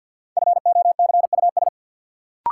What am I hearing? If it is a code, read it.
Morse code, heard as a keyed tone, about 50 wpm, 700 Hz. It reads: FOCUS
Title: Sentences from Top 1000 Words - Review Part 4 - New Word Drill - 50wpm